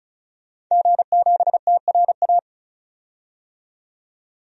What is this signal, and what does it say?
Morse code, heard as a keyed tone, about 34 wpm, 700 Hz. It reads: G7TRA
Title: DXCC-100 Call Signs - 5 Characters - 35wpm